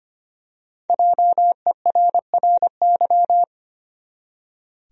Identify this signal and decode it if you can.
Morse code, heard as a keyed tone, about 25 wpm, 700 Hz. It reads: JERRY
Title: USA Common Names - Max 5 Letters - 25wpm